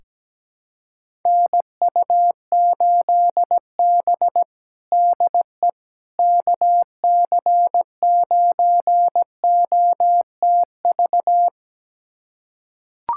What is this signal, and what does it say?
Morse code, heard as a keyed tone, about 17 wpm, 700 Hz. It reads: NU8B DE KC9OTV
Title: USA Callsign DE Callsign 17wpm